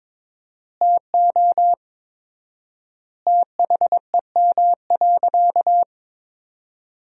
Morse code, heard as a keyed tone, about 22 wpm, 700 Hz. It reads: TO THEM.
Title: 4x Word-Spacing - Sentences from Top 500 Words - 22wpm